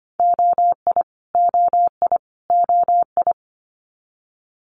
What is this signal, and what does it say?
Morse code, heard as a keyed tone, about 25 wpm, 700 Hz. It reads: OI OI OI
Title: Multi-letter Phonemes Repeated 3x 25wpm